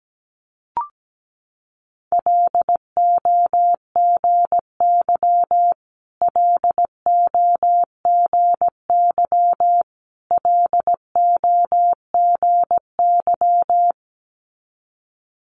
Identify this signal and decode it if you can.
Morse code, heard as a keyed tone, about 17 wpm, 700 Hz. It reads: LOGY LOGY LOGY
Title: Multi-letter Phonemes Repeated 3x 17wpm